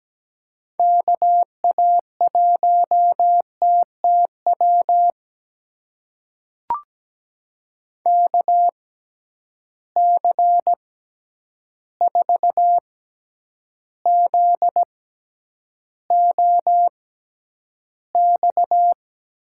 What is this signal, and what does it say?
Morse code, heard as a keyed tone, about 17 wpm, spaced as for 9 wpm, 700 Hz. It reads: KA1TTW KC4ZOX
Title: USA Call Signs - 6 Characters - 6x Character Spacing - 17wpm